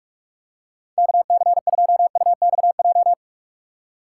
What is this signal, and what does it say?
Morse code, heard as a keyed tone, about 48 wpm, 700 Hz. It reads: KC2UXJ